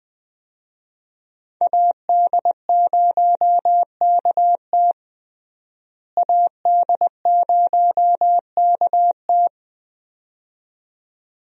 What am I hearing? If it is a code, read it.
Morse code, heard as a keyed tone, about 20 wpm, 700 Hz. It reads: AD0KT AD0KT
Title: USA Call Signs - 5 Characters - Repeated 2x - 20wpm